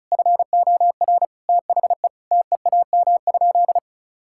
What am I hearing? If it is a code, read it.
Morse code, heard as a keyed tone, about 36 wpm, 700 Hz. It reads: FOR THE TEAM?
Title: Sentences from Top 1600 Words 35wpm